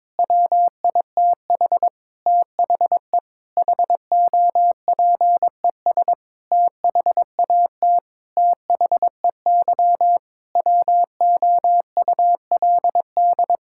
Morse code, heard as a keyed tone, about 22 wpm, 700 Hz. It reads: WITH THE HOPES THAT THEY WOULD
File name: Sentences with 9-Word Phrases - 22wpm